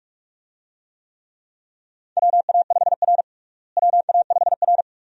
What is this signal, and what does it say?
Morse code, heard as a keyed tone, about 42 wpm, 700 Hz. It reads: WA5R WA5R
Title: USA Call Signs - 4 Characters - Repeated 2x - 45wpm